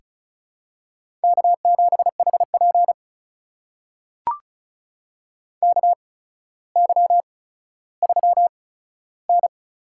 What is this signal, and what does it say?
Morse code, heard as a keyed tone, about 36 wpm, spaced as for 15 wpm, 700 Hz. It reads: K7HP KY3N
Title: USA Call Signs - 4 Characters - 8x Character Spacing - 35wpm